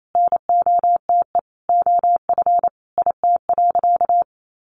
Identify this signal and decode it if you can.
Morse code, heard as a keyed tone, about 28 wpm, 700 Hz. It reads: NOTE OF IT.
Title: Sentences from Top 500 Words 28wpm